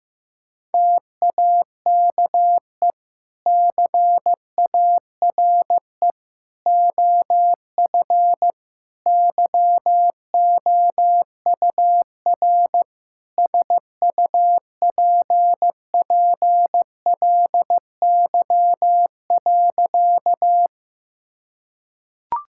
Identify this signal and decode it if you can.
Morse code, heard as a keyed tone, about 15 wpm, 700 Hz. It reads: TAKE CARE OF YOUR SUPPLY.
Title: Sentences from Top 1000 Words - Review Part 5 - 15wpm